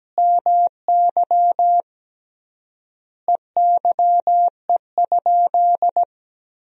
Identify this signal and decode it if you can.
Morse code, heard as a keyed tone, about 17 wpm, 700 Hz. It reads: MY EYE?
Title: 3x Word-Spacing - Sentences from Top 500 Words - 17wpm